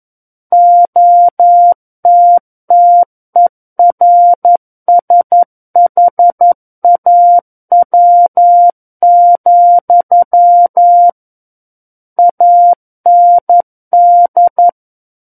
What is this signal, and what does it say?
Morse code, heard as a keyed tone, about 11 wpm, 700 Hz. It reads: OTTERSHAW, AND
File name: 11 WPM